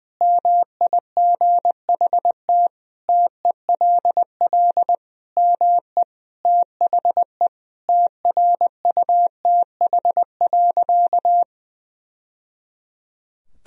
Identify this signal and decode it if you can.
Morse code, heard as a keyed tone, about 20 wpm, 700 Hz. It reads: MIGHT TELL ME THE TRUTH.
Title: Sentences from Top 1000 Words - Review Part 1 - 20wpm